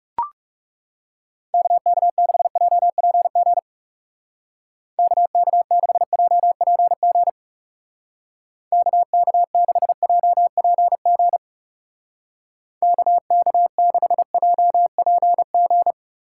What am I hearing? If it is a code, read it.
Morse code, heard as a keyed tone, about 36 wpm, 700 Hz. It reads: KK6JPG KK6JPG KK6JPG KK6JPG
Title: Speed Racing - USA Call Signs 30wpm